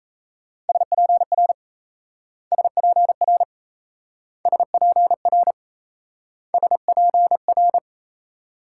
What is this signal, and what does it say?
Morse code, heard as a keyed tone, about 37 wpm, 700 Hz. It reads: SPR SPR SPR SPR